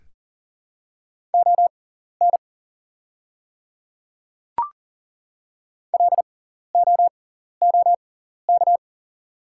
Morse code, heard as a keyed tone, about 37 wpm, spaced as for 13 wpm, 700 Hz. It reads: ON LOOK
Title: 6x Character Spacing - Top 100 Words - 40wpm